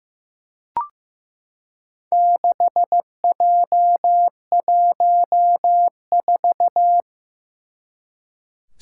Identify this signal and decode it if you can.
Morse code, heard as a keyed tone, about 15 wpm, 700 Hz. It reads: 6J14
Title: Four Challenging Letter-Number Combinations 15wpm